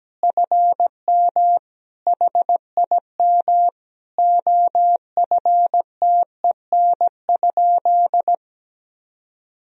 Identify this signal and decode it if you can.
Morse code, heard as a keyed tone, about 17 wpm, 700 Hz. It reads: FM HIM OFTEN?
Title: Sentences from Top 400 Words and Abbreviations 17wpm